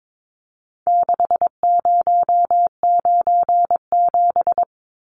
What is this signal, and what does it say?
Morse code, heard as a keyed tone, about 22 wpm, 700 Hz. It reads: 6097
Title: Numbers - 4-Digit 22wpm